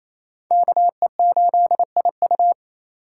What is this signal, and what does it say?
Morse code, heard as a keyed tone, about 36 wpm, 700 Hz. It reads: KE8IU